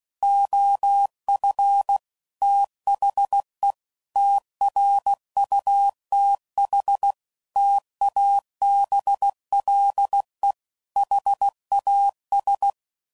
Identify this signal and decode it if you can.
Morse code, heard as a keyed tone, about 16 wpm, 800 Hz. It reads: OF THE TRUTH TABLE HAS
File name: CW Code Groups @15WPM / Technical Subjects / AND Gate